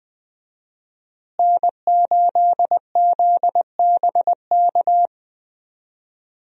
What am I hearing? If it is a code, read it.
Morse code, heard as a keyed tone, about 20 wpm, 700 Hz. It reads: N8ZBK
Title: USA Call Signs - 5 Characters - Repeated 3x - 20wpm